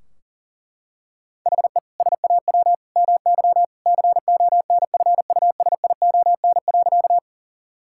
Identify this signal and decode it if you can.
Morse code, heard as a keyed tone, about 40 wpm, 700 Hz. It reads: HE SAW MY CONFUSION.
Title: Sentences from Top 1600 Words 40wpm